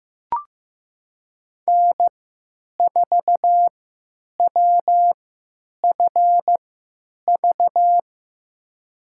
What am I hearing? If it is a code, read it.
Morse code, heard as a keyed tone, about 15 wpm, spaced as for 9 wpm, 700 Hz. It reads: N4WFV
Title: USA Call Signs - 3x Character Spacing - 15wpm